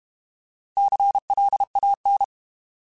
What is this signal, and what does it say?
Morse code, heard as a keyed tone, about 32 wpm, 800 Hz. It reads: CLAN
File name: Alphabetical / Alphabetical mp3s / Cs @30